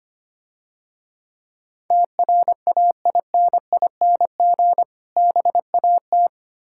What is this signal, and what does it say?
Morse code, heard as a keyed tone, about 28 wpm, 700 Hz. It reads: TRAINING BAT
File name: practice / 25wpm / Difficult Words - Speed-Racing with Hints - 25wpm